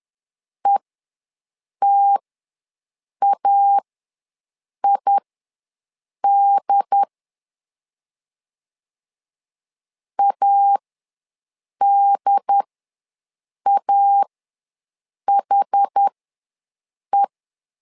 Morse code, heard as a keyed tone, about 10 wpm, spaced as for 6 wpm, 800 Hz. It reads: ETAID ADAHE
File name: LEZIONI / 2 UHDN / 2 UHD ws 10wpm 2058